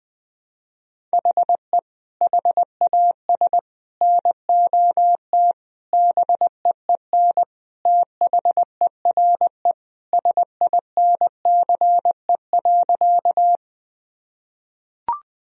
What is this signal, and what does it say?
Morse code, heard as a keyed tone, about 20 wpm, 700 Hz. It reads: HE HAS NOT BEEN THERE SINCE.
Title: Sentences from Top 1000 Words - Review Part 1 - 20wpm